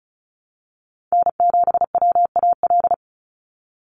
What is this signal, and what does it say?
Morse code, heard as a keyed tone, about 36 wpm, 700 Hz. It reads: N7WAL
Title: USA Call Signs 35wpm